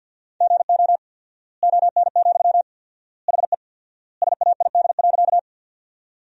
Morse code, heard as a keyed tone, about 50 wpm, 700 Hz. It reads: GO ON, HE SAID.
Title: 4x Word-Spacing - Sentences from Top 100 Words - 50wpm